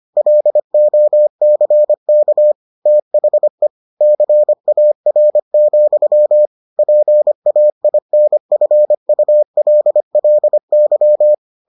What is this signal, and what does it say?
Morse code, heard as a keyed tone, about 25 wpm, 600 Hz. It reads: LOCK THE CAR, PAINFULLY